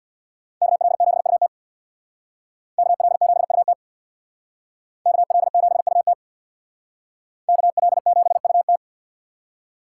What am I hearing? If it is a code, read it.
Morse code, heard as a keyed tone, about 62 wpm, 700 Hz. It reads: KL7UT KL7UT KL7UT KL7UT
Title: Speed Racing - USA 2x2 Call Signs 50wpm